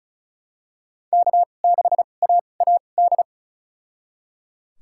Morse code, heard as a keyed tone, about 36 wpm, spaced as for 25 wpm, 700 Hz. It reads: K6AAD
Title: USA Call Signs - 5 Characters - 2x Character Spacing - 35wpm